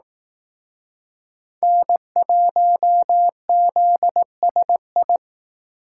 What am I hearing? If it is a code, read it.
Morse code, heard as a keyed tone, about 19 wpm, 700 Hz. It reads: N1ZSI